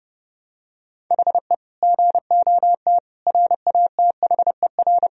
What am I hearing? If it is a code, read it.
Morse code, heard as a keyed tone, about 30 wpm, 700 Hz. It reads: HE GOT RATHER
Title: Sentences from Top 1000 Words - Review Part 4 - 30wpm